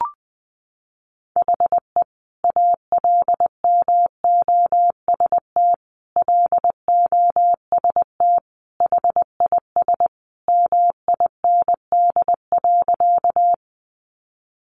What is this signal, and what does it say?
Morse code, heard as a keyed tone, about 20 wpm, 700 Hz. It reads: HE ALMOST LOST HIS MIND.
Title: Sentences from Top 1400 Words 20wpm